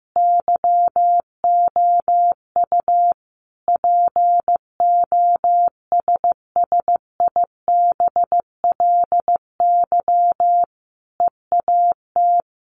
Morse code, heard as a keyed tone, about 15 wpm, 700 Hz. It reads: YOU POSSIBLY EAT
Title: Sentences from Top 1300 Words 15wpm